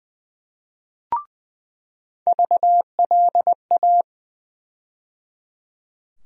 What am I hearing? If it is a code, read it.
Morse code, heard as a keyed tone, about 20 wpm, 700 Hz. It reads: VLA
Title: USA Call Sign Postfixes - 3 Letters 20wpm